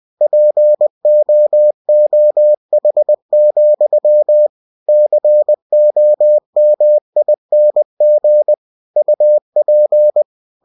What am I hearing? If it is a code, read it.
Morse code, heard as a keyed tone, about 20 wpm, 600 Hz. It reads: POOH, COMING UP
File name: cw-books / winnie-the-pooh / winnie-20wpm-page-0088